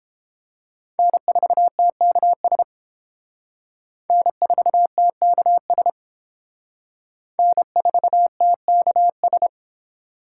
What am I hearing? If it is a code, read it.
Morse code, heard as a keyed tone, about 31 wpm, 700 Hz. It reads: N4TKS N4TKS N4TKS